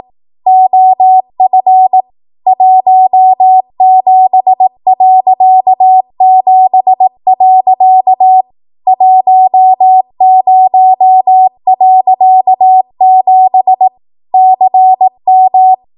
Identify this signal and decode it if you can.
Morse code, heard as a keyed tone, about 18 wpm, 750 Hz. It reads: OF 17.7. 10.7 CM
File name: Archive / 18 WPM / 18WPM